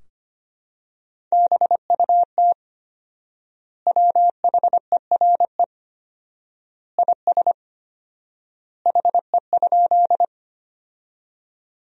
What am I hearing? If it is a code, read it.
Morse code, heard as a keyed tone, about 25 wpm, 700 Hz. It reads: BUT WHERE IS HE?
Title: 4x Word-Spacing - Sentences from Top 200 Words - 25wpm